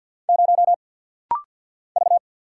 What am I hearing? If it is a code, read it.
Morse code, heard as a keyed tone, about 48 wpm, 700 Hz. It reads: 0 V
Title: Single Letter-Number - Rapid-Fire 50wpm